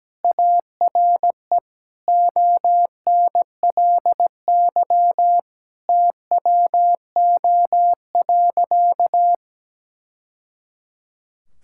Morse code, heard as a keyed tone, about 17 wpm, 700 Hz. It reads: ARE ONLY TWO.
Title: Sentences from Top 1000 Words - Review Part 5 - 17wpm